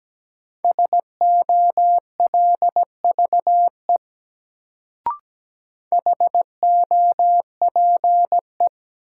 Morse code, heard as a keyed tone, about 17 wpm, 700 Hz. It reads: SOLVE HOPE
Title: Sentences from Top 1000 Words - New Word Drill - 17wpm